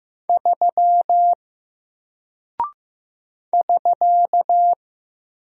Morse code, heard as a keyed tone, about 15 wpm, 700 Hz. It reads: 3 <SK>
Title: Single Character (Letter:Number:Punctuation:Prosign) 15wpm